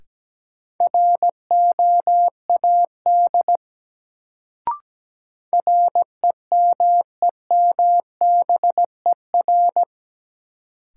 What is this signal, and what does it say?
Morse code, heard as a keyed tone, about 17 wpm, 700 Hz. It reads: ROAD REMEMBER
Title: Sentences from Top 400 Words - New Word Drill - 17wpm